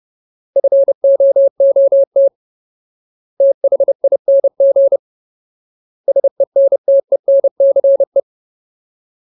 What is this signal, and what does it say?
Morse code, heard as a keyed tone, about 30 wpm, 550 Hz. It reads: FOOT THING SENTENCE
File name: Ultimate Word List / 30WPM Ultimate